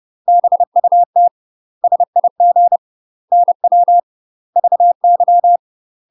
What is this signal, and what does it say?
Morse code, heard as a keyed tone, about 30 wpm, 700 Hz. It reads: BUT SIG NW VY
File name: UK MASTER QSOs TRAINING / 30WPM / UK MASTER QSOs 0027